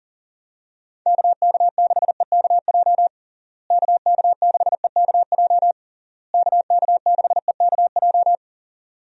Morse code, heard as a keyed tone, about 40 wpm, 700 Hz. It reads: KK6EKJ KK6EKJ KK6EKJ
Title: USA Call Signs - 6 Characters - Repeated 3x - 40wpm